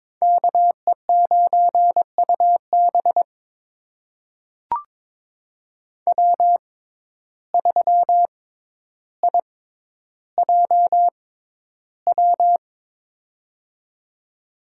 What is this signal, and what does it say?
Morse code, heard as a keyed tone, about 22 wpm, spaced as for 11 wpm, 700 Hz. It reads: KE9UB W3IJW
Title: USA Call Signs - 5 Characters - 6x Character Spacing - 22wpm